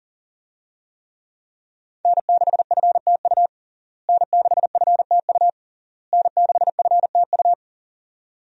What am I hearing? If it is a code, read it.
Morse code, heard as a keyed tone, about 37 wpm, 690 Hz. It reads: N6FTU N6FTU N6FTU
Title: USA Call Signs - 5 Characters - Repeated 3x - 40wpm